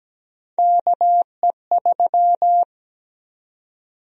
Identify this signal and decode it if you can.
Morse code, heard as a keyed tone, about 17 wpm, 700 Hz. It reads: KE3
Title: USA Call Sign Prefixes - 17wpm